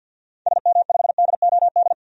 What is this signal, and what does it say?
Morse code, heard as a keyed tone, about 49 wpm, 700 Hz. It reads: SM5DOD
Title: DXCC-100 Call Signs - 6 Characters - 50wpm